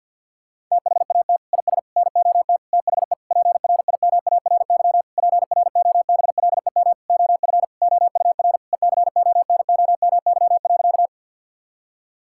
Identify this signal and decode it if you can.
Morse code, heard as a keyed tone, about 51 wpm, 700 Hz. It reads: THAT IS NOT THE PRIMARY PROBLEM OF OUR ECONOMY.
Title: Sentences from Top 1000 Words - Review Part 5 - 50wpm